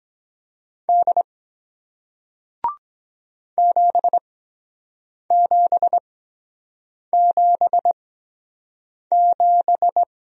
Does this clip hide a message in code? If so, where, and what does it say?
Yes: Morse code from 0.9 to 10.0 s, heard as a keyed tone, about 24 wpm, spaced as for 7 wpm, 700 Hz. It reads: D 7777